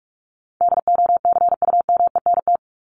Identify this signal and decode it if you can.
Morse code, heard as a keyed tone, about 44 wpm, 700 Hz. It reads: DOCUMENT
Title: Top 1751-2000 Words 45wpm